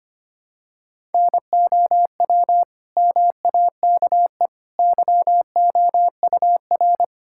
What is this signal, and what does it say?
Morse code, heard as a keyed tone, about 25 wpm, 700 Hz. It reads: NOW MAKE YOUR
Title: Sentences from Top 1000 Words - Review Part 6 - 25wpm